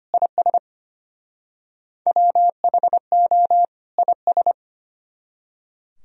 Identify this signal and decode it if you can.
Morse code, heard as a keyed tone, about 32 wpm, 700 Hz. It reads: IS WHO IS